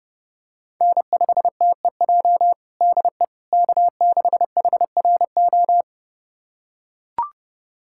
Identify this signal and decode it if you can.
Morse code, heard as a keyed tone, about 30 wpm, 700 Hz. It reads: N5TEJ DE K6HRO
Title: USA Callsign DE Callsign 30wpm